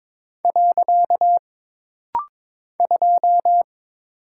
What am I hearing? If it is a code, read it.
Morse code, heard as a keyed tone, about 22 wpm, 700 Hz. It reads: . 2
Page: This practice set is composed of Letters, Numbers, common punctuations, and common prosigns.